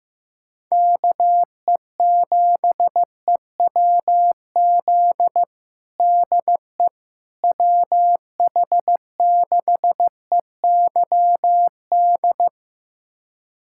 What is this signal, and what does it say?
Morse code, heard as a keyed tone, about 15 wpm, 700 Hz. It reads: KE7EWZ DE WH6EYD
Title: USA Callsign DE Callsign 15wpm